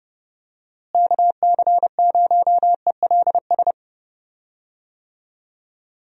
Morse code, heard as a keyed tone, about 30 wpm, 700 Hz. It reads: KC0ELS
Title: USA Call Signs - 6 Characters - Repeated 3x - 30wpm